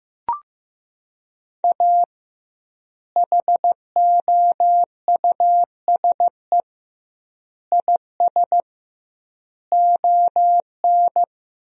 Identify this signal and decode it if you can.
Morse code, heard as a keyed tone, about 16 wpm, 700 Hz. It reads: A HOUSE IS ON
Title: 2x Word-Spacing - Sentences from Top 400 Words - 15wpm